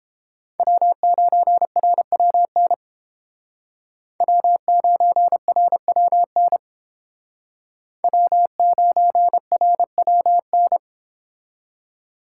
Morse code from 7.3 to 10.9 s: W9RWN

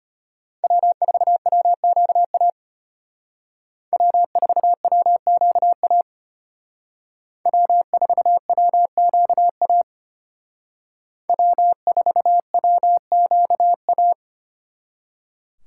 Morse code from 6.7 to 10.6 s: W4WQA

